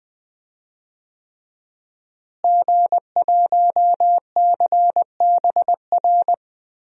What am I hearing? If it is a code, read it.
Morse code, heard as a keyed tone, about 20 wpm, 700 Hz. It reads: G1CBR
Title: DXCC-100 Call Signs - 5 Characters - Repeated 3x - 20wpm